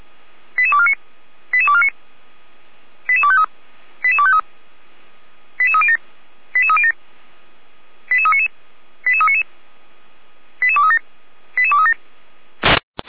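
Five-tone selective call (ZVEI), heard as quick runs of five tones, tones of about 70 ms. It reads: 88249 88249 88253 88253 88307 88307 88399 88399 88237 88237
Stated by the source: ZVEI five-tone selective call received on an SDR, from the Feuerwehr Orlamünde audio archive.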